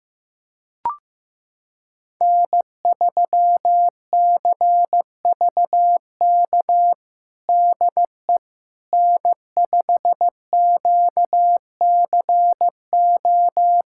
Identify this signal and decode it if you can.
Morse code, heard as a keyed tone, about 15 wpm, 700 Hz. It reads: N3CVK DE N5QCO
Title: USA Callsign DE Callsign - 5 Characters - 15wpm